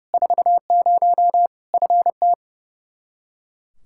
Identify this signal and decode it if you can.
Morse code, heard as a keyed tone, about 30 wpm, 700 Hz. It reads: 40 FT
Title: Practice QSO - Exchange 2 - Info - 30wpm